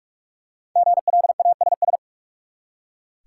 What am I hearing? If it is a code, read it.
Morse code, heard as a keyed tone, about 46 wpm, 700 Hz. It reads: GLASS
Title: Five Letter Words - 45wpm